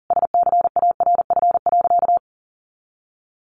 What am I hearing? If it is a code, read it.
Morse code, heard as a keyed tone, about 37 wpm, 700 Hz. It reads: SCARF.
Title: Easy Phrases 40wpm